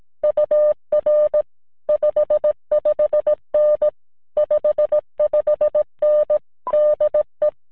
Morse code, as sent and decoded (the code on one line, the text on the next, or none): ..- .-. / ..... ..... -. / ..... ..... -. / -.. .
UR 55N 55N DE